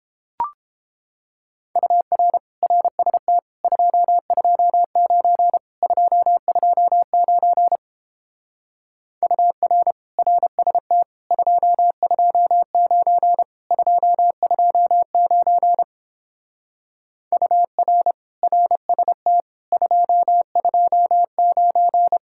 Morse code, heard as a keyed tone, about 29 wpm, 700 Hz. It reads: UR RST 229 229 UR RST 229 229 UR RST 229